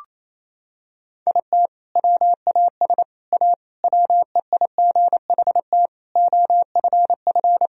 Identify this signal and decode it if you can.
Morse code, heard as a keyed tone, about 27 wpm, 700 Hz. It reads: IT WAS A WEIGHT OFF
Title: Sentences from Top 1000 Words - Review Part 3 - 28wpm